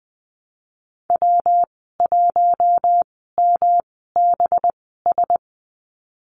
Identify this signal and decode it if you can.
Morse code, heard as a keyed tone, about 20 wpm, spaced as for 15 wpm, 700 Hz. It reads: W1MBS
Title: 238 Lesson 38g - Letter q - Call Signs 20wpm